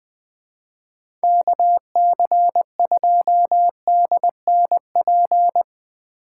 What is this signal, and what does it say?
Morse code, heard as a keyed tone, about 21 wpm, 700 Hz. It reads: KC2DNP
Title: USA Call Signs - 6 Characters - 4x Character Spacing - 20wpm